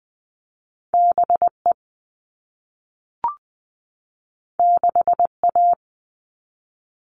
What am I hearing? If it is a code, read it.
Morse code, heard as a keyed tone, about 20 wpm, 700 Hz. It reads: BE 6A
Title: All Cut Numbers - 2-Digit 20wpm